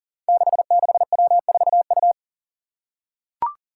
Morse code, heard as a keyed tone, about 40 wpm, 700 Hz. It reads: 66W4U
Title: Five Challenging Letter-Number Combinations 40wpm